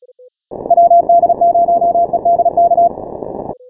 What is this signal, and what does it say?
Morse code, heard as a keyed tone, about 36 wpm, 690 Hz. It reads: WD0EDK
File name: final / W1UJ-168-140-MorseRunner